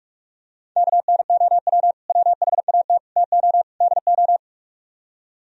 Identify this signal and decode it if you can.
Morse code, heard as a keyed tone, about 44 wpm, 700 Hz. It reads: KNOW WHAT TO DO